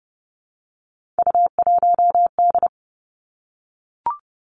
Morse code, heard as a keyed tone, about 30 wpm, 700 Hz. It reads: U1D
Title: Three Challenging Letter-Number Combinations 30wpm